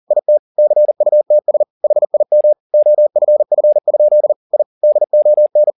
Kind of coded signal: Morse code